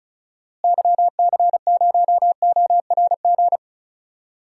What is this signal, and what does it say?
Morse code, heard as a keyed tone, about 36 wpm, 700 Hz. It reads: YC0ORG
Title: DXCC-100 Call Signs - 6 Characters - Repeated 2x - 35wpm